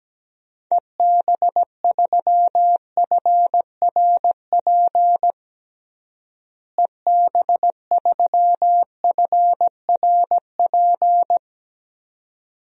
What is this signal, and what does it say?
Morse code, heard as a keyed tone, about 17 wpm, 700 Hz. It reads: EB3FRP EB3FRP
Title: DXCC-100 Call Signs - 6 Characters - Repeated 2x - 17wpm